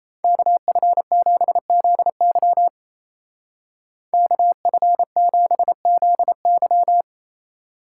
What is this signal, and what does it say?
Morse code, heard as a keyed tone, about 37 wpm, 700 Hz. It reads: KF7ZY KF7ZY